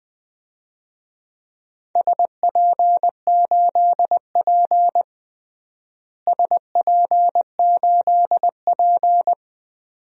Morse code, heard as a keyed tone, about 20 wpm, 700 Hz. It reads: SP8P SP8P
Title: DXCC-100 Call Signs - 4 Characters - Repeated 2x - 20wpm